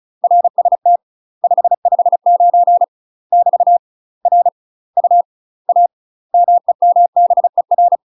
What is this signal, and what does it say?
Morse code, heard as a keyed tone, about 36 wpm, 700 Hz. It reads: RST 559 <BT> R U A MEMBER
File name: UK MASTER QSOs TRAINING / 35WPM / UK MASTER QSOs 0011